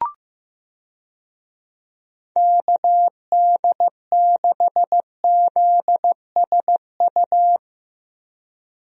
Morse code, heard as a keyed tone, about 15 wpm, 700 Hz. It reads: KD6ZSU
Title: USA Call Signs - 6 Characters - Repeated 2x - 15wpm